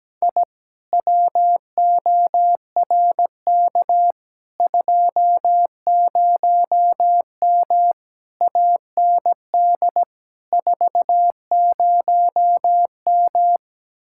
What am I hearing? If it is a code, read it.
Morse code, heard as a keyed tone, about 17 wpm, 700 Hz. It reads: I WORK 20M AND 40M
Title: Speed-Racing Common QSO Phrases - 17wpm